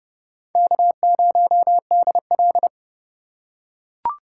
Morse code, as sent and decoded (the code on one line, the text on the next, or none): -.- ----- -.. .-..
K0DL